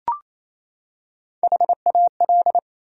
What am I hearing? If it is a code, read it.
Morse code, heard as a keyed tone, about 28 wpm, 700 Hz. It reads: HAL
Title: USA Common Names - Max 5 Letters - 28wpm